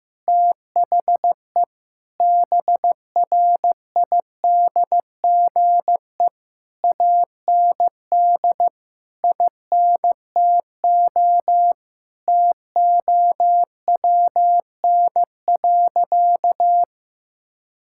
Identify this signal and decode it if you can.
Morse code, heard as a keyed tone, about 15 wpm, 700 Hz. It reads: THE BRIDGE AND INTO TOWN.
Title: Sentences from Top 1600 Words 15wpm